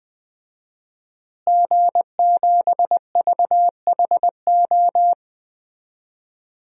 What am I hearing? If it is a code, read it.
Morse code, heard as a keyed tone, about 21 wpm, 700 Hz. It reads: G7VHO
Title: DXCC-100 Call Signs - 5 Characters - Repeated 2x - 20wpm